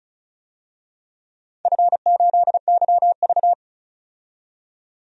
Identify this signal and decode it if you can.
Morse code, heard as a keyed tone, about 36 wpm, 700 Hz. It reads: F8YV
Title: DXCC-100 Call Signs - 4 Characters - 35wpm